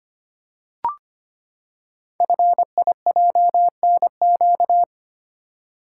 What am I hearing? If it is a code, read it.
Morse code, heard as a keyed tone, about 25 wpm, 700 Hz. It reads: FIJNQ
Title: Code Groups - 5 Letters - 25wpm